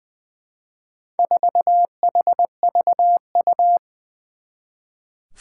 Morse code, heard as a keyed tone, about 20 wpm, 700 Hz. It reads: 4HVU
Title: Four Challenging Letter-Number Combinations 20wpm